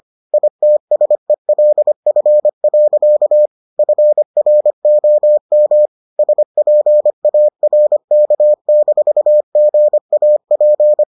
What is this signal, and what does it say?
Morse code, heard as a keyed tone, about 25 wpm, 600 Hz. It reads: ITSELF. FROM SPARK-GAP